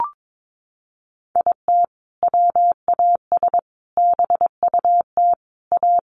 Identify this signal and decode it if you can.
Morse code, heard as a keyed tone, about 22 wpm, 700 Hz. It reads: IT WAS BUT A
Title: Sentences from Top 1500 Words 22wpm